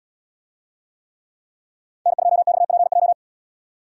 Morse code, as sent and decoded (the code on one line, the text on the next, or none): -. ..--- -..- .--. .---
N2XPJ